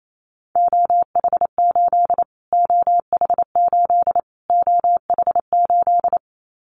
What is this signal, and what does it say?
Morse code, heard as a keyed tone, about 28 wpm, 700 Hz. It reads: OH8 OH8 OH8
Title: Callsign Prefixes for Top 25 DXCC - Repeated 3x - 28wpm